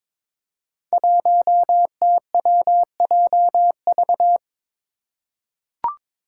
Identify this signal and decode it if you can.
Morse code, heard as a keyed tone, about 21 wpm, 700 Hz. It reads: 1TWJV
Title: Five Challenging Letter-Number Combinations 22wpm